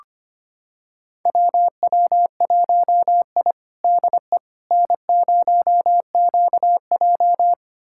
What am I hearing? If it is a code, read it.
Morse code, heard as a keyed tone, about 25 wpm, 700 Hz. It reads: WW1I DE N0QJ
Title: USA Callsign DE Callsign - 4 Characters - 25wpm